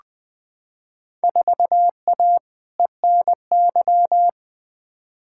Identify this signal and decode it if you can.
Morse code, heard as a keyed tone, about 20 wpm, 700 Hz. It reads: 4A ENY
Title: Field Day Exchanges Encore - 20wpm